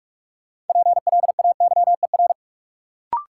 Morse code, heard as a keyed tone, about 46 wpm, 700 Hz. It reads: PLAYER
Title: Sentences from Top 1000 Words - Review Part 4 - New Word Drill - 45wpm